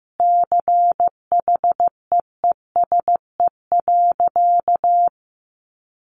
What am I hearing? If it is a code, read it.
Morse code, heard as a keyed tone, about 15 wpm, 700 Hz. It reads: CHEESE.
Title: Easy Phrases 15wpm